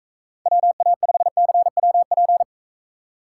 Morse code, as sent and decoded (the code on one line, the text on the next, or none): .-- .- ..... -.-. .-- .--.
WA5CWP